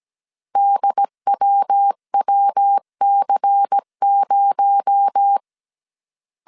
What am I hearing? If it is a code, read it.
Morse code, heard as a keyed tone, about 17 wpm, 800 Hz. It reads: DWWC0
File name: CW / DET / 1 6 1500C S C 16wpm 2733